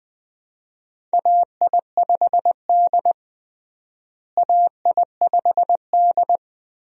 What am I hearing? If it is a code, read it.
Morse code, heard as a keyed tone, about 20 wpm, 700 Hz. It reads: AI5D AI5D
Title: USA Call Signs - 4 Characters - Repeated 3x - 20wpm